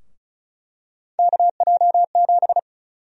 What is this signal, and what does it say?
Morse code, heard as a keyed tone, about 36 wpm, 700 Hz. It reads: KJ7
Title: USA Call Sign Prefixes - Repeated 2x - 35wpm